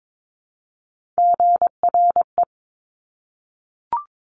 Morse code, heard as a keyed tone, about 22 wpm, 700 Hz. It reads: GRE
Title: Top Three Letter Combinations in Words 22wpm